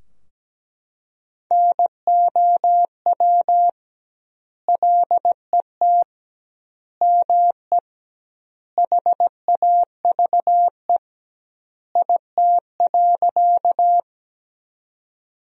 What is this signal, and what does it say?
Morse code, heard as a keyed tone, about 17 wpm, 700 Hz. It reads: NOW LET ME HAVE IT.
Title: 2x Word-Spacing - Sentences from Top 300 Words - 17wpm